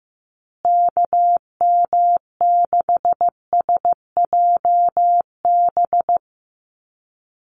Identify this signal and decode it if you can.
Morse code, heard as a keyed tone, about 15 wpm, 700 Hz. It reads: KM6SJB
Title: USA Call Signs Repeated 3x 15wpm